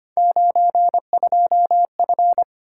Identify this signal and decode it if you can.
Morse code, heard as a keyed tone, about 25 wpm, 700 Hz. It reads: 92F
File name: Three Letter-Number Combinations - 25wpm